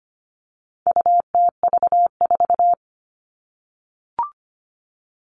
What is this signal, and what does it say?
Morse code, heard as a keyed tone, about 25 wpm, 700 Hz. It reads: UTV4
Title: All Cut Numbers - 4-Digit 25wpm